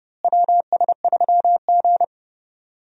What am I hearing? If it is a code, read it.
Morse code, heard as a keyed tone, about 30 wpm, 700 Hz. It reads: WS3G
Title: USA Call Signs - 4 Characters - Repeated 3x - 30wpm